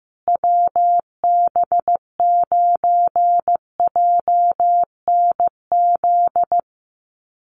Morse code, heard as a keyed tone, about 15 wpm, 700 Hz. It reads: WB9JNZ
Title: USA Call Signs Repeated 2x 15wpm